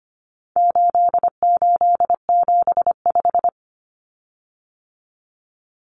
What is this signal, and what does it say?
Morse code, heard as a keyed tone, about 25 wpm, 700 Hz. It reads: 8875
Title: Common Cut Numbers - 4-Digit 25wpm